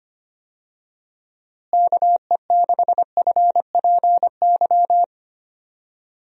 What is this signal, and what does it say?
Morse code, heard as a keyed tone, about 25 wpm, 700 Hz. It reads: KE6FPY
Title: USA Call Signs - 6 Characters - Repeated 3x - 25wpm